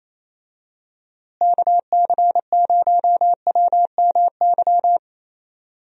Morse code, heard as a keyed tone, about 28 wpm, 700 Hz. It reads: KC0WMY
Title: USA Call Signs - 6 Characters - Repeated 2x - 28wpm